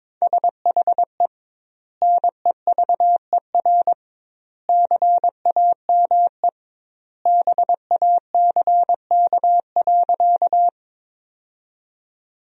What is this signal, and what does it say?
Morse code, heard as a keyed tone, about 21 wpm, 700 Hz. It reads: SHE NEVER CAME BACK.
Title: 2x Word-Spacing - Sentences from Top 300 Words - 22wpm